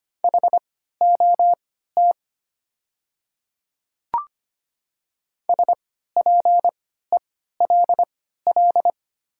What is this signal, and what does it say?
Morse code, heard as a keyed tone, about 25 wpm, spaced as for 14 wpm, 700 Hz. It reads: HOT SPELL
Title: 3x Character Spacing - Top 200 Words - 25wpm